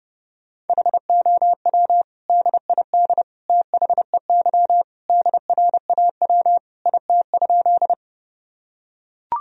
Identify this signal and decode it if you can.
Morse code, heard as a keyed tone, about 31 wpm, 700 Hz. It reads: HOW DID THEY DRAW IT?
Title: Sentences from Top 1000 Words - Review Part 1 - 30wpm